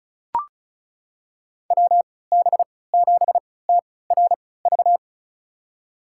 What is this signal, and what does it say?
Morse code, heard as a keyed tone, about 36 wpm, spaced as for 20 wpm, 700 Hz. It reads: WB7TRV
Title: USA Call Signs - 6 Characters - 3x Character Spacing - 35wpm